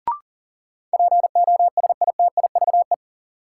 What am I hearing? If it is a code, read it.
Morse code, heard as a keyed tone, about 40 wpm, 700 Hz. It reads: POSITIVE
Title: Sentences from Top 1000 Words - Review Part 5 - New Word Drill - 40wpm